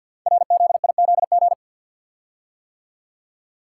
Morse code, heard as a keyed tone, about 50 wpm, 700 Hz. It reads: RZIZG
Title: Code Groups - 5 Letters - 50wpm